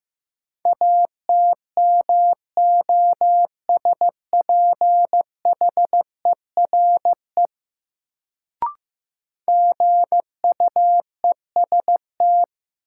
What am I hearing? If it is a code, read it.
Morse code, heard as a keyed tone, about 15 wpm, 700 Hz. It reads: ATMOSPHERE GUEST
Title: Sentences from Top 1600 Words - New Word Drill - 15wpm